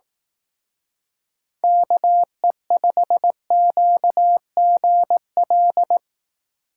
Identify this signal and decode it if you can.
Morse code, heard as a keyed tone, about 18 wpm, 700 Hz. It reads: KE5QGL
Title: Speed Racing - USA 2x3 Call Signs 15wpm